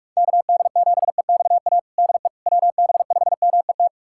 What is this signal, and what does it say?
Morse code, heard as a keyed tone, about 46 wpm, 700 Hz. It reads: KD7EXA DE WB5MET
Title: USA Callsign DE Callsign - 6 Characters - 45wpm